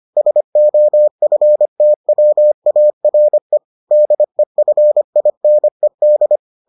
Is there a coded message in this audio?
Yes: Morse code from 0.2 to 6.4 s, heard as a keyed tone, about 25 wpm, 600 Hz. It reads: SOFTWARE DEFINED